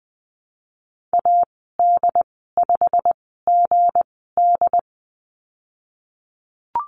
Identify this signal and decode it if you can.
Morse code, heard as a keyed tone, about 20 wpm, spaced as for 15 wpm, 700 Hz. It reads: AD5GD
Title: 179 Lesson 29g - Slash - Call Signs 20wpm